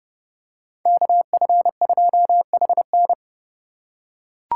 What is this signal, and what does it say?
Morse code, heard as a keyed tone, about 28 wpm, 700 Hz. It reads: KF2HN